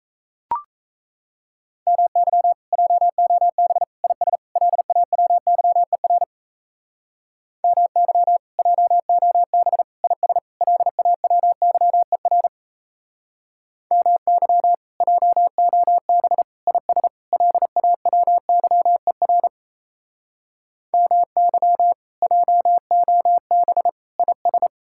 Morse code, heard as a keyed tone, about 35 wpm, 700 Hz. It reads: MY JOB IS LAWYER MY JOB IS LAWYER MY JOB IS LAWYER MY JOB IS
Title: Speed-Racing Common QSO Phrases - 28wpm